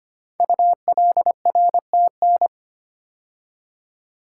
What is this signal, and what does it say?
Morse code, heard as a keyed tone, about 25 wpm, 700 Hz. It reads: ULRTN
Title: Code Groups - 5 Letters - 25wpm